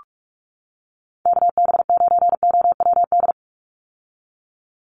Morse code, heard as a keyed tone, about 44 wpm, 700 Hz. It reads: KB9OWD
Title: Call Signs - 45wpm